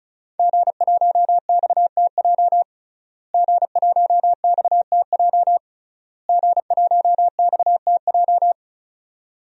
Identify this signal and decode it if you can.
Morse code, heard as a keyed tone, about 34 wpm, 700 Hz. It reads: G1XTJ G1XTJ G1XTJ
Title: DXCC-100 Call Signs - 5 Characters - Repeated 3x - 35wpm